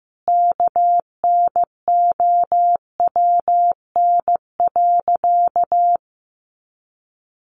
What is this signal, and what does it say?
Morse code, heard as a keyed tone, about 15 wpm, 700 Hz. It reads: KNOWN.
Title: Sentences from Top 1300 Words 15wpm